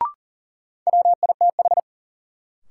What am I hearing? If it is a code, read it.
Morse code, heard as a keyed tone, about 38 wpm, 700 Hz. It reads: WITH
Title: Sets of 2 Words - New Word Drill - 40wpm